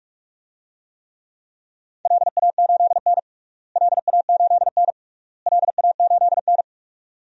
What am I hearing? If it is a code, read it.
Morse code, heard as a keyed tone, about 47 wpm, 700 Hz. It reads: LA8N LA8N LA8N
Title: DXCC-100 Call Signs - 4 Characters - Repeated 3x - 45wpm